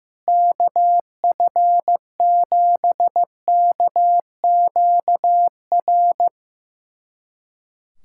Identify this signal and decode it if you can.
Morse code, heard as a keyed tone, about 16 wpm, 700 Hz. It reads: KF7KQR
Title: USA Callsign DE Callsign 15wpm